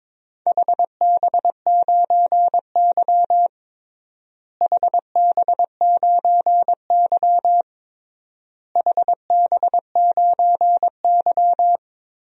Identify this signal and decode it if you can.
Morse code, heard as a keyed tone, about 22 wpm, 700 Hz. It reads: HB9Y HB9Y HB9Y
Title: DXCC-100 Call Signs - 4 Characters - Repeated 3x - 22wpm